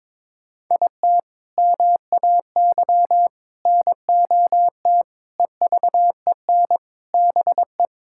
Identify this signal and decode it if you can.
Morse code, heard as a keyed tone, about 22 wpm, 700 Hz. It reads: IT MAY NOT EVEN BE
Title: Sentences from Top 1000 Words - Review Part 6 - 22wpm